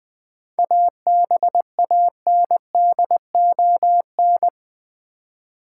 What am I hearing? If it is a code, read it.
Morse code, heard as a keyed tone, about 20 wpm, 700 Hz. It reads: ABANDON
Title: 7 Letter Words - First 3 Letters Given - 20wpm